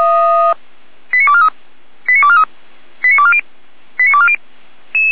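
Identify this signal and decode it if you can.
Five-tone selective call (ZVEI), heard as quick runs of five tones, tones of about 70 ms. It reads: 89243 89243 89240 89240
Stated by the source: ZVEI five-tone selective call received via SDR.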